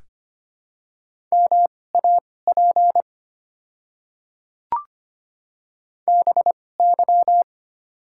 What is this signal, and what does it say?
Morse code, heard as a keyed tone, about 25 wpm, spaced as for 18 wpm, 690 Hz. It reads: MAP BY
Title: 2x Character Spacing - Top 500 Words - 25wpm